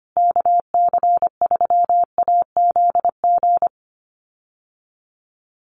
Morse code, heard as a keyed tone, about 25 wpm, 700 Hz. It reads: KC3AZG
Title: USA Call Signs Repeated 2x 25wpm